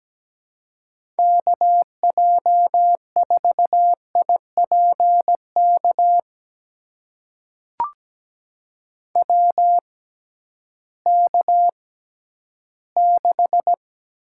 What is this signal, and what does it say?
Morse code, heard as a keyed tone, about 17 wpm, spaced as for 11 wpm, 700 Hz. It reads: KJ4IPK WK6
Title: USA Call Signs - 6x Character Spacing - 17wpm